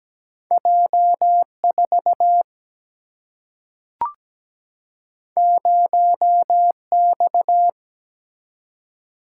Morse code, heard as a keyed tone, about 17 wpm, 700 Hz. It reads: J4 0X
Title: Single-Letter Single-Number 17wpm